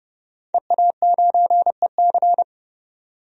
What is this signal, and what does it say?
Morse code, heard as a keyed tone, about 30 wpm, 700 Hz. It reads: EA9EC
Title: DXCC-100 Call Signs - 5 Characters - 30wpm